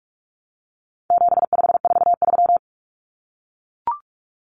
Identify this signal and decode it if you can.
Morse code, heard as a keyed tone, about 44 wpm, 700 Hz. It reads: 7543
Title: Common Cut Numbers - 4-Digit 45wpm